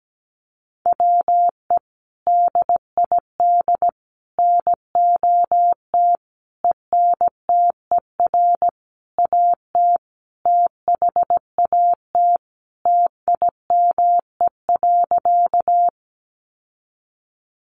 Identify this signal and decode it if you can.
Morse code, heard as a keyed tone, about 17 wpm, 700 Hz. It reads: WE DID NOT ENTER AT THAT TIME.
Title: Sentences from Top 1300 Words 17wpm